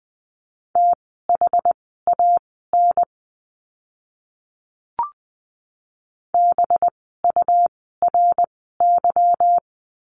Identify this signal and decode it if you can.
Morse code, heard as a keyed tone, about 20 wpm, spaced as for 15 wpm, 700 Hz. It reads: THAN BURY
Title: 185 Lesson 30f - Letter b - Words 20wpm